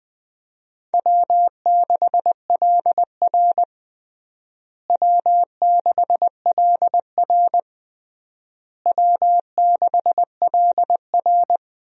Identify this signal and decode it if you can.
Morse code, heard as a keyed tone, about 20 wpm, 700 Hz. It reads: W6LR W6LR W6LR
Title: USA Call Signs - 4 Characters - Repeated 3x - 20wpm